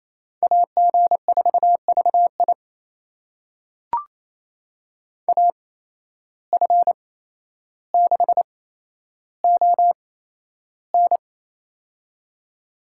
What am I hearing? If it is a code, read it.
Morse code, heard as a keyed tone, about 28 wpm, spaced as for 12 wpm, 700 Hz. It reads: AG4VI AF6ON
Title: USA Call Signs - 5 Characters - 8x Character Spacing - 28wpm